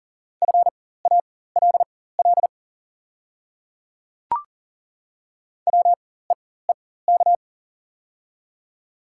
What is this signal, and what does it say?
Morse code, heard as a keyed tone, about 40 wpm, spaced as for 19 wpm, 700 Hz. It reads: FALL WEEK
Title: 4x Character Spacing - Top 500 Words - 40wpm